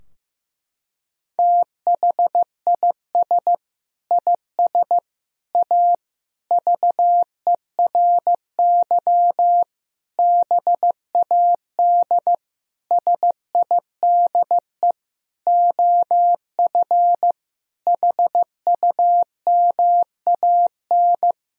Morse code, heard as a keyed tone, about 15 wpm, 700 Hz. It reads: THIS IS A VERY BAD SIDE OF HUMAN
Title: Sentences from Top 1000 Words - Review Part 3 - 15wpm